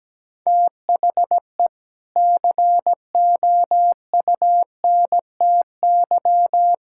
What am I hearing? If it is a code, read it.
Morse code, heard as a keyed tone, about 17 wpm, 700 Hz. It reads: THE COUNTY
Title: Sentences from Top 1600 Words 17wpm